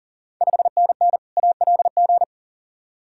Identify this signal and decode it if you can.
Morse code, heard as a keyed tone, about 38 wpm, 690 Hz. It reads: 5NN ALG
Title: Ontario QSO Party Exchange 40wpm